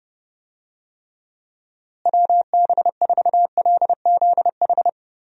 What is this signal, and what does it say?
Morse code, heard as a keyed tone, about 30 wpm, 700 Hz. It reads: WB4LZH
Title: USA Call Signs - 6 Characters - Repeated 3x - 30wpm